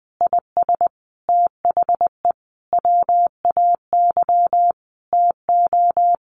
Morse code, heard as a keyed tone, about 20 wpm, 700 Hz. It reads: IS THE WAY TO
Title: Sentences from Top 1400 Words 20wpm